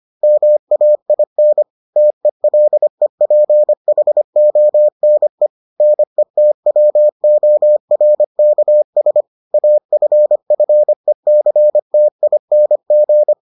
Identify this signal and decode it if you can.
Morse code, heard as a keyed tone, about 25 wpm, 600 Hz. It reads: MAIN TELEPHONE NETWORKS AFFECTING